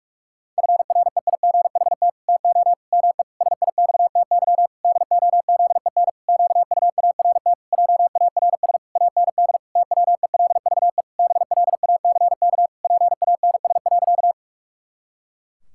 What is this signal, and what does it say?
Morse code, heard as a keyed tone, about 41 wpm, 700 Hz. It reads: FREIGHT TO ME SIXTY DOZEN QUART JARS AND TWELVE BLACK PANS.
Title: Pangram Challenge v2 - 45wpm